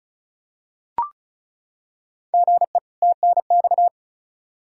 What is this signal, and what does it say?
Morse code, heard as a keyed tone, about 34 wpm, 700 Hz. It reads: GE TNX
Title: Practice QSO - Exchange 1 - Beginning - 35wpm